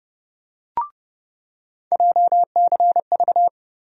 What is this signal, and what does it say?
Morse code, heard as a keyed tone, about 31 wpm, 700 Hz. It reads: JCV
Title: USA Call Sign Postfixes - 3 Letters 30wpm